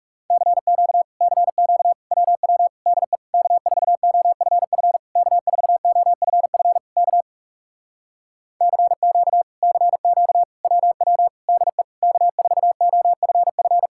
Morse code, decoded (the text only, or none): CQ CQ WW DE K4OFF K4OFF K CQ CQ WW DE K4OFF